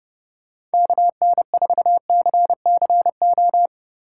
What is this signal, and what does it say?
Morse code, heard as a keyed tone, about 28 wpm, 700 Hz. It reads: KN4CCO